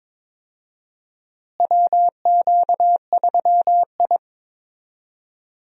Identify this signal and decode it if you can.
Morse code, heard as a keyed tone, about 22 wpm, 700 Hz. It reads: WQ3I
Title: USA Call Signs - 4 Characters - Repeated 2x - 22wpm